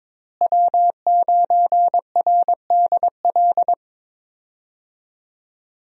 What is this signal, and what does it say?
Morse code, heard as a keyed tone, about 22 wpm, 700 Hz. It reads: W9RDL
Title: USA Call Signs - 5 Characters - Repeated 2x - 22wpm